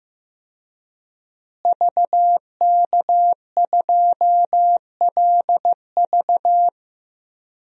Morse code, heard as a keyed tone, about 15 wpm, 700 Hz. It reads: VK2LV
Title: DXCC-100 Call Signs - 5 Characters - Repeated 2x - 15wpm